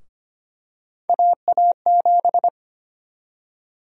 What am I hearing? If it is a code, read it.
Morse code, heard as a keyed tone, about 25 wpm, 700 Hz. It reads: AA7
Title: USA Call Sign Prefixes - 25wpm